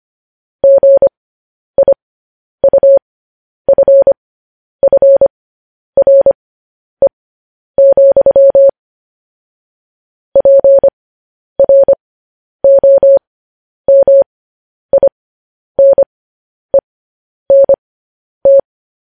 Morse code, heard as a keyed tone, about 25 wpm, spaced as for 10 wpm, 550 Hz. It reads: GIUFFRE, PROMINENT
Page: news headlines in morse code at 10 wpm